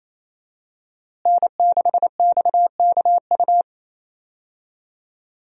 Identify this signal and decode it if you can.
Morse code, heard as a keyed tone, about 28 wpm, 700 Hz. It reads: N6XKU
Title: USA Call Signs - 5 Characters - Repeated 3x - 28wpm